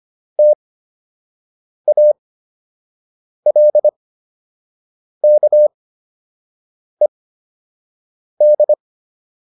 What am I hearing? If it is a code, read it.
Morse code, heard as a keyed tone, about 25 wpm, spaced as for 6 wpm, 600 Hz. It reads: TALKED